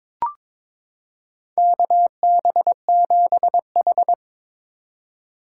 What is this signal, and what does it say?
Morse code, heard as a keyed tone, about 21 wpm, 700 Hz. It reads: KB7H
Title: K1USN SST QSO Practice - 22wpm